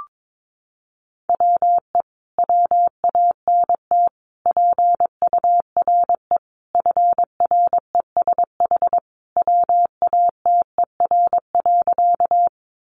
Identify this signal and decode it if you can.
Morse code, heard as a keyed tone, about 22 wpm, 700 Hz. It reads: WE WANT PURE FRESH WATER.
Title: Sentences from Top 1500 Words 22wpm